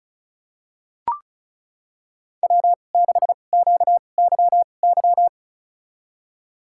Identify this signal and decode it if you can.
Morse code, heard as a keyed tone, about 36 wpm, spaced as for 25 wpm, 700 Hz. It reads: W6QYY
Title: USA Call Signs - 5 Characters - 2x Character Spacing - 35wpm